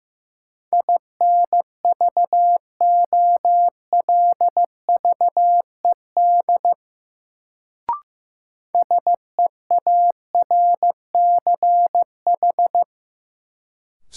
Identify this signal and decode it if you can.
Morse code, heard as a keyed tone, about 15 wpm, 700 Hz. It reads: INVOLVED SEARCH
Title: Sentences from Top 800 Words - New Word Drill - 15wpm